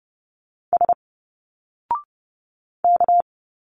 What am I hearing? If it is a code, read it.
Morse code, heard as a keyed tone, about 30 wpm, 700 Hz. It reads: S K
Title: Single Letters 30wpm